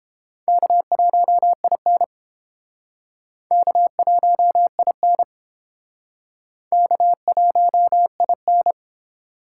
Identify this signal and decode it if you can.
Morse code, heard as a keyed tone, about 30 wpm, 700 Hz. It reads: K1IN K1IN K1IN